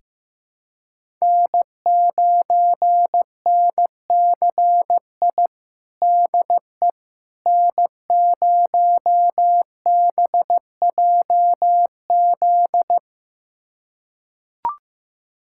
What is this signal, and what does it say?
Morse code, heard as a keyed tone, about 15 wpm, 700 Hz. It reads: N9NCI DE N0BJZ
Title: USA Callsign DE Callsign - 5 Characters - 15wpm